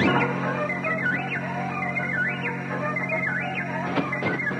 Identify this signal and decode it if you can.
Emergency Alert System (SAME) tones, heard as harsh buzzing bursts. It is heard from 0.8 to 3.4 s.